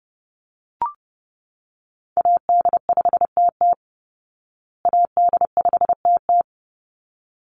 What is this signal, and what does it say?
Morse code, heard as a keyed tone, about 29 wpm, 700 Hz. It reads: AD5TT AD5TT
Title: Call Signs Repeated 3x - 30wpm